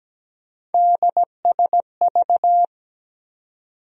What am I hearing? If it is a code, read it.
Morse code, heard as a keyed tone, about 17 wpm, 700 Hz. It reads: DSV
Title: USA Call Sign Postfixes - 3 Letters 17wpm